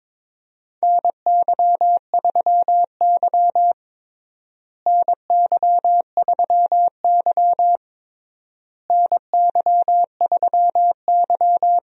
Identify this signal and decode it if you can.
Morse code, heard as a keyed tone, about 22 wpm, 700 Hz. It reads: NY3Y NY3Y NY3Y
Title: USA Call Signs - 4 Characters - Repeated 3x - 22wpm